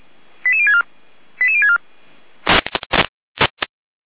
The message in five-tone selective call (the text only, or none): 88064 88064